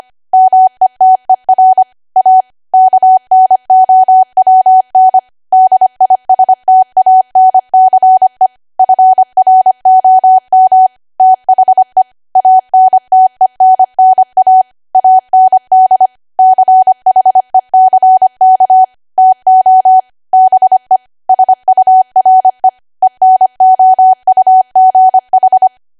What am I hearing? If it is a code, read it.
Morse code, heard as a keyed tone, about 25 wpm, 750 Hz. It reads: METER A KNOWN DISTANCE FROM THE ANTENNA AND CHECK TO BE SURE ENOUGH
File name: Archive / 25 WPM / 25WPM